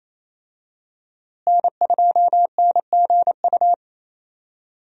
Morse code, heard as a keyed tone, about 28 wpm, 700 Hz. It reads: N2NGU
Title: USA Call Signs - 5 Characters - Repeated 3x - 28wpm